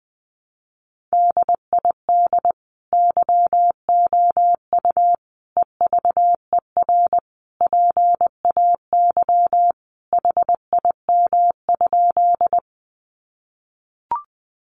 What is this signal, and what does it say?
Morse code, heard as a keyed tone, about 20 wpm, 700 Hz. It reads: DID YOU EVER PAY HIM?
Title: Sentences from Top 700 Words 20wpm